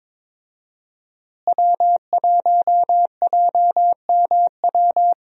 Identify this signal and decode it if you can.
Morse code, heard as a keyed tone, about 22 wpm, 700 Hz. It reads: W1JMW
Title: USA Call Signs - 5 Characters - Repeated 3x - 22wpm